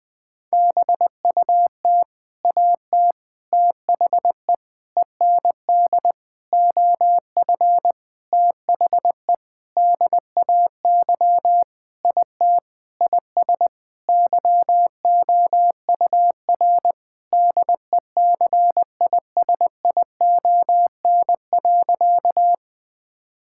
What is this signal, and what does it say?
Morse code, heard as a keyed tone, about 20 wpm, 700 Hz. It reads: BUT AT THE END OF THE DAY IT IS YOUR DECISION.
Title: Sentences with 8-Word Phrases - 20wpm